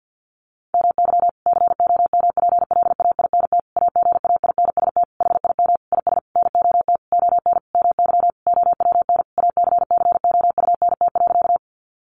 Morse code, heard as a keyed tone, about 50 wpm, 700 Hz. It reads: MY COMPLAINT AGAINST HIM IS NOT ON MY OWN ACCOUNT.
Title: Sentences from Top 1400 Words 50wpm